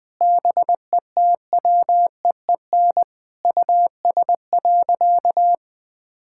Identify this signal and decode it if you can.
Morse code, heard as a keyed tone, about 20 wpm, 700 Hz. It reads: BETWEEN US.
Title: Sentences from Top 1600 Words 20wpm